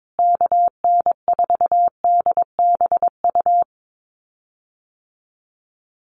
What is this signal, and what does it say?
Morse code, heard as a keyed tone, about 22 wpm, 700 Hz. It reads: KN4DBU
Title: USA Call Signs Repeated 3x 22wpm